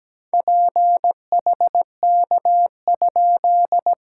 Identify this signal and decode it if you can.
Morse code, heard as a keyed tone, about 17 wpm, 690 Hz. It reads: PHK?
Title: SOTA QSO Practice 17wpm